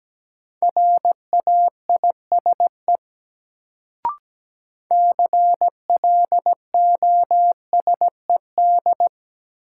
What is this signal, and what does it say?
Morse code, heard as a keyed tone, about 17 wpm, 700 Hz. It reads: RAISE CLOSED
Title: Sentences from Top 1400 Words - New Word Drill - 17wpm